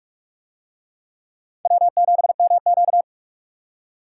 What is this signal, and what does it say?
Morse code, heard as a keyed tone, about 44 wpm, 700 Hz. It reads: W7MQ